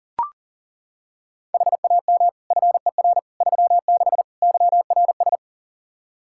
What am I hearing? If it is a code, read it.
Morse code, heard as a keyed tone, about 37 wpm, 700 Hz. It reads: HAM FER 36 YRS
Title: K1USN SST QSO Practice 40wpm